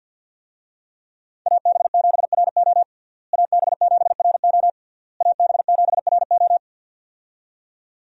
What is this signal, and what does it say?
Morse code, heard as a keyed tone, about 52 wpm, 700 Hz. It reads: AB7RO AB7RO AB7RO
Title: USA Call Signs - 5 Characters - Repeated 3x - 50wpm